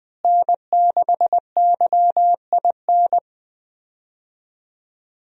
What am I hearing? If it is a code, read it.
Morse code, heard as a keyed tone, about 20 wpm, 700 Hz. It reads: N6YIN
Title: USA Call Signs - 5 Characters - Repeated 2x - 20wpm